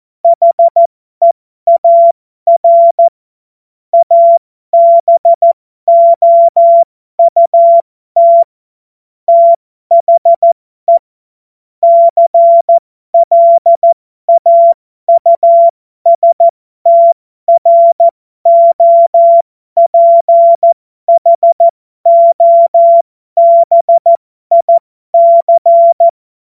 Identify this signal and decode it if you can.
Morse code, heard as a keyed tone, about 14 wpm, spaced as for 12 wpm, 680 Hz. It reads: HEAR ABOUT THE CLAUSTROPHOBIC
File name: Morsekurs / Witze